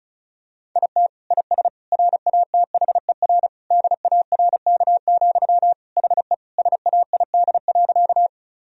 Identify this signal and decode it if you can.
Morse code, heard as a keyed tone, about 36 wpm, 700 Hz. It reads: IT IS RATHER DARK, HE SAID.
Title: Sentences from Top 1000 Words - Review Part 4 - 35wpm